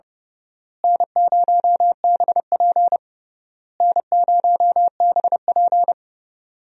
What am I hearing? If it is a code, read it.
Morse code, heard as a keyed tone, about 32 wpm, 700 Hz. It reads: N0BP N0BP
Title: USA Call Signs - 4 Characters - Repeated 3x - 30wpm